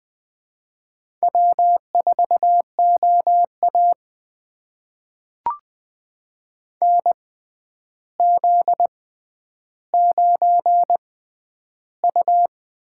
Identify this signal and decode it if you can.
Morse code, heard as a keyed tone, about 20 wpm, spaced as for 10 wpm, 700 Hz. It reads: W4OA NZ9U
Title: USA Call Signs - 4 Characters - 6x Character Spacing - 20wpm